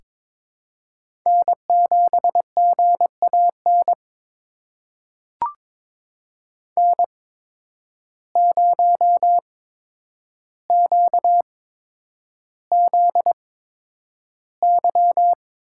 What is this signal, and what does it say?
Morse code, heard as a keyed tone, about 23 wpm, spaced as for 10 wpm, 700 Hz. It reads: N7GAN N0QZY
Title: USA Call Signs - 5 Characters - 8x Character Spacing - 22wpm